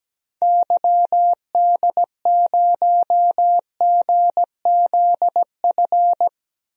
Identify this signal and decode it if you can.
Morse code, heard as a keyed tone, about 17 wpm, 700 Hz. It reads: YD0GZF
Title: DXCC-100 Call Signs - 6 Characters - Repeated 3x - 17wpm